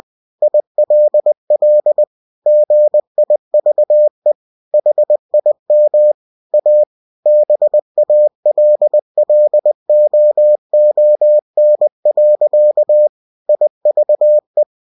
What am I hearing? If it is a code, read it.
Morse code, heard as a keyed tone, about 20 wpm, 600 Hz. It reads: ILL GIVE HIM A BALLOON. IVE
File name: cw-books / winnie-the-pooh / winnie-20wpm-page-0049